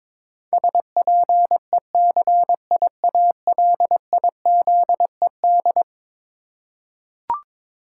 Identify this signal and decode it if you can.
Morse code, heard as a keyed tone, about 22 wpm, 700 Hz. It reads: SPECIALIZED
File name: practice / 22wpm / Eleven Letter Words - 22wpm